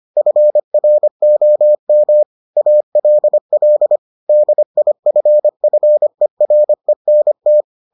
Morse code, heard as a keyed tone, about 25 wpm, 600 Hz. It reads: FROM ALL DIFFERENT